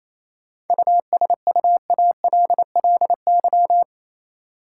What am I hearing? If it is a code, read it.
Morse code, heard as a keyed tone, about 28 wpm, 700 Hz. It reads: USUALLY
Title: Sentences from Top 1000 Words - Review Part 3 - New Word Drill - 28wpm